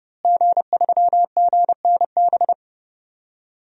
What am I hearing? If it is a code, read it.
Morse code, heard as a keyed tone, about 31 wpm, 700 Hz. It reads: G3GNB
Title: DXCC-100 Call Signs - 5 Characters - Repeated 3x - 30wpm